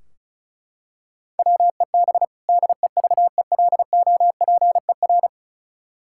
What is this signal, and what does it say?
Morse code, heard as a keyed tone, about 36 wpm, 700 Hz. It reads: WEB DEVELOPER
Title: Practice QSO - Exchange 3 - More Info - 35wpm